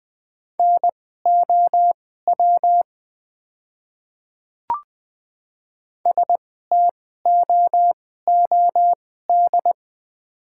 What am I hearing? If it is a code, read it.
Morse code, heard as a keyed tone, about 20 wpm, spaced as for 15 wpm, 700 Hz. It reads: NOW STOOD
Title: 2x Character Spacing - Top 500 Words - 20wpm